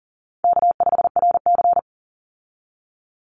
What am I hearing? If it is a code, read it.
Morse code, heard as a keyed tone, about 40 wpm, 700 Hz. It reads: K5RC
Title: Call Signs Repeated 3x - 40wpm